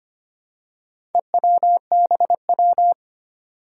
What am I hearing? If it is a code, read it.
Morse code, heard as a keyed tone, about 25 wpm, 700 Hz. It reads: EWBW
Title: Four Challenging Letter-Number Combinations 25wpm